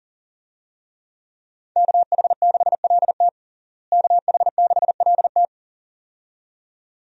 Morse code, heard as a keyed tone, about 40 wpm, 700 Hz. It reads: KH6LT KH6LT
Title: USA Call Signs - 5 Characters - Repeated 2x - 40wpm